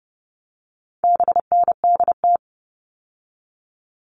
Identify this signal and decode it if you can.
Morse code, heard as a keyed tone, about 30 wpm, 700 Hz. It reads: BNDT
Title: All Cut Numbers - 4-Digit 30wpm